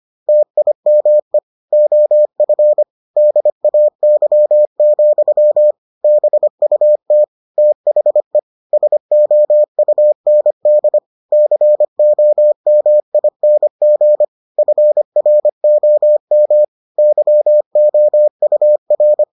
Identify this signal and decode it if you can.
Morse code, heard as a keyed tone, about 25 wpm, 600 Hz. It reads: TIME OF DAY, BUT THE SOUND COMING FROM YOUR